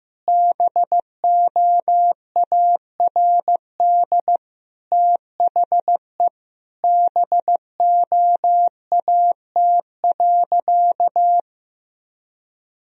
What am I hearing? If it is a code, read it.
Morse code, heard as a keyed tone, about 15 wpm, 700 Hz. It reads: BOARD THE BOAT.
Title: Sentences from Top 1000 Words - Review Part 4 - 15wpm